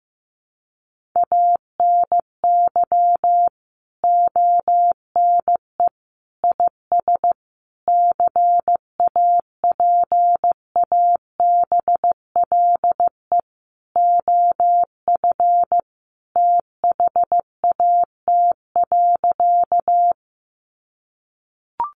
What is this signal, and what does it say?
Morse code, heard as a keyed tone, about 15 wpm, 700 Hz. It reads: ANY ONE IS CAPABLE OF THAT.
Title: Sentences from Top 1200 Words 15wpm